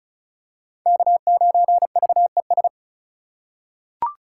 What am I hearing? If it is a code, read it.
Morse code, heard as a keyed tone, about 36 wpm, 700 Hz. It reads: K9VES
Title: USA Call Signs - 5 Characters - 4x Character Spacing - 35wpm